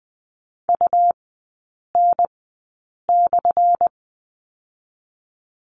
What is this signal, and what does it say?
Morse code, heard as a keyed tone, about 20 wpm, spaced as for 8 wpm, 700 Hz. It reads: UN/
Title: 178 Lesson 29e - Slash - Three Characters 20wpm